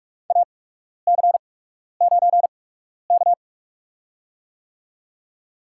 Morse code, heard as a keyed tone, about 45 wpm, spaced as for 12 wpm, 700 Hz. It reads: AC9K
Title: USA Call Signs - 4 Characters - 8x Character Spacing - 45wpm